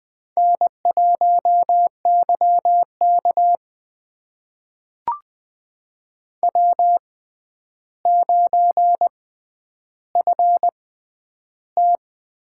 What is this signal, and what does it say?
Morse code, heard as a keyed tone, about 20 wpm, spaced as for 10 wpm, 700 Hz. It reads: N1YK W9FT
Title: USA Call Signs - 4 Characters - 6x Character Spacing - 20wpm